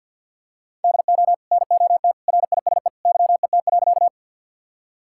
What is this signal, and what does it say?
Morse code, heard as a keyed tone, about 52 wpm, 700 Hz. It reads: DO NOT RISE YET.
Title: Sentences from Top 1000 Words - Review Part 6 - 50wpm